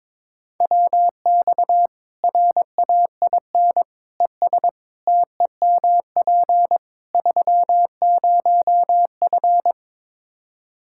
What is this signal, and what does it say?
Morse code, heard as a keyed tone, about 22 wpm, 700 Hz. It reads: WX RAIN ES TEMP 30F
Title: K1USN SST QSO Practice 22wpm